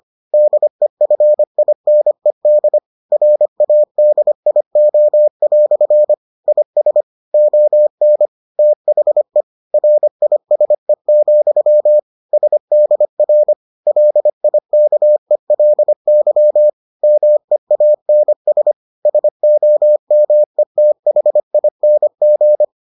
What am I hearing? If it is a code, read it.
Morse code, heard as a keyed tone, about 25 wpm, 600 Hz. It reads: DEFINED RADIO" IS ON THE RISE, SDR LIKELY MEANS SOMETHING